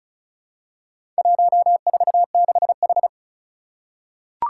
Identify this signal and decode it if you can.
Morse code, heard as a keyed tone, about 36 wpm, 700 Hz. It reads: 146H
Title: Four Challenging Letter-Number Combinations 35wpm